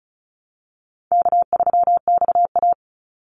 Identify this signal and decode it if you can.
Morse code, heard as a keyed tone, about 34 wpm, 700 Hz. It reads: K3XA
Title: Call Signs Repeated 2x - 35wpm